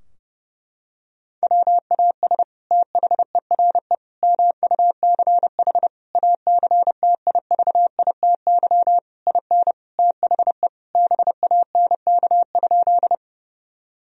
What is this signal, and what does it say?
Morse code, heard as a keyed tone, about 30 wpm, 700 Hz. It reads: WAS THERE MUCH ACTIVITY IN THE BANK?
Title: Sentences from Top 1000 Words - Review Part 4 - 30wpm